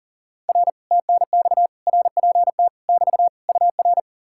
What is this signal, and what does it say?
Morse code, heard as a keyed tone, about 58 wpm, spaced as for 49 wpm, 700 Hz. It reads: R TNX RPT <BT> UR
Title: Speed-Racing Common QSO Phrases - 40wpm